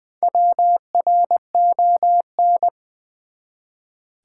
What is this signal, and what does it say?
Morse code, heard as a keyed tone, about 20 wpm, 700 Hz. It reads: WRON
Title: Build A Word - 1 Letter At A Time - 5-Letter Words - 20wpm